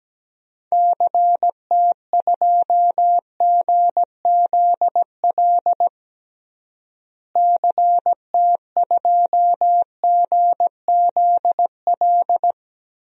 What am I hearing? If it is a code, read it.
Morse code, heard as a keyed tone, about 17 wpm, 700 Hz. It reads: CT2GZL CT2GZL
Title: DXCC-100 Call Signs - 6 Characters - Repeated 2x - 17wpm